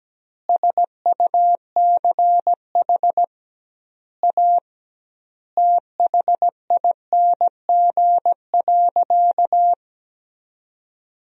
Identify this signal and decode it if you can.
Morse code, heard as a keyed tone, about 17 wpm, 700 Hz. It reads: SUCH A THING.
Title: 2x Word-Spacing - Sentences from Top 200 Words - 17wpm